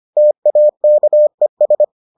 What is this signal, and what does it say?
Morse code, heard as a keyed tone, about 25 wpm, 600 Hz. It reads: TAKES